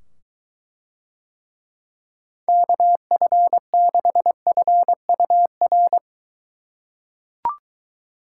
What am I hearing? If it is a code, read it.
Morse code, heard as a keyed tone, about 21 wpm, 700 Hz. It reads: KF6FUR